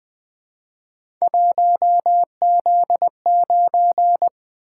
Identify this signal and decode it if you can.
Morse code, heard as a keyed tone, about 20 wpm, 700 Hz. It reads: 1Z9
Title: Three Letter-Number Combinations - 20wpm